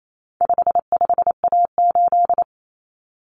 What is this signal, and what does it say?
Morse code, heard as a keyed tone, about 28 wpm, 700 Hz. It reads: 55A8
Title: Common Cut Numbers - 4-Digit 28wpm